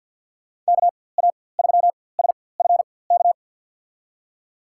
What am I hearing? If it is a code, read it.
Morse code, heard as a keyed tone, about 52 wpm, spaced as for 23 wpm, 700 Hz. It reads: KA3SFK
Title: USA Call Signs - 4x Character Spacing - 50wpm